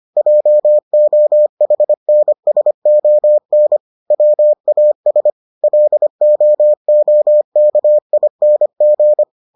Morse code, heard as a keyed tone, about 25 wpm, 600 Hz. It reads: JOHNSON WAS LOOKING